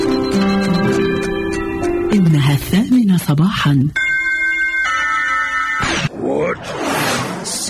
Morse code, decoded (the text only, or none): none